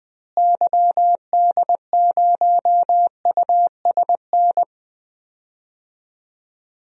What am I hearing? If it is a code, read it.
Morse code, heard as a keyed tone, about 20 wpm, 700 Hz. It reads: YD0USN
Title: DXCC-100 Call Signs - 6 Characters - Repeated 3x - 20wpm